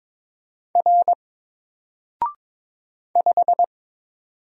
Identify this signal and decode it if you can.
Morse code, heard as a keyed tone, about 22 wpm, 700 Hz. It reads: R 5
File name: Single-Letter-Number-22wpm